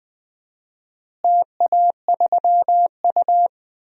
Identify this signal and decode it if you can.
Morse code, heard as a keyed tone, about 20 wpm, 700 Hz. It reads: TA3U
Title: DXCC-100 Call Signs - 4 Characters - Repeated 2x - 20wpm